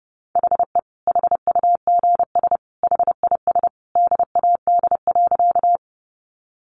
Morse code, heard as a keyed tone, about 30 wpm, 700 Hz. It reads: HE HUGS HIS DAD.
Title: Easy Phrases 30wpm